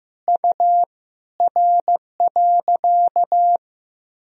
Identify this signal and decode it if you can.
Morse code, heard as a keyed tone, about 15 wpm, 700 Hz. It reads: U R.
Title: Sentences from Top 500 Words and Abbreviations 15wpm